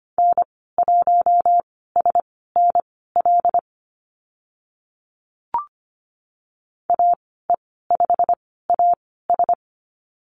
Morse code, heard as a keyed tone, about 25 wpm, spaced as for 16 wpm, 700 Hz. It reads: N1SNL AE5AS
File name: Lesson 15g - Number 5 - Call Signs 25wpm